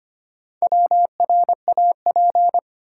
Morse code, heard as a keyed tone, about 25 wpm, 700 Hz. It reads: WRAP
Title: Four Letter Words - 25wpm